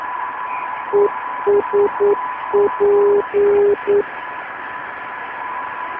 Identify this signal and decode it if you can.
Morse code, heard as a keyed tone, about 9 wpm, 410 Hz, with noise. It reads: ESP